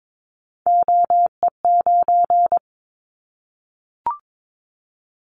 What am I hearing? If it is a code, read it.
Morse code, heard as a keyed tone, about 22 wpm, 700 Hz. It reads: OE9
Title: Callsign Prefixes for Top 25 DXCC - 22wpm